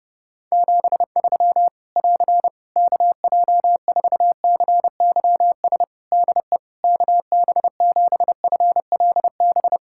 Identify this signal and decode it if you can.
Morse code, heard as a keyed tone, about 30 wpm, 700 Hz. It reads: 73 <AR> KJ4CYS DE KB7FLB
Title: Practice QSO - Exchange 4 - Ending - 30wpm